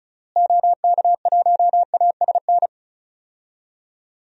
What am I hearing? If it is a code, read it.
Morse code, heard as a keyed tone, about 36 wpm, 700 Hz. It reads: OK1ASN
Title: DXCC-100 Call Signs - 6 Characters - Repeated 3x - 35wpm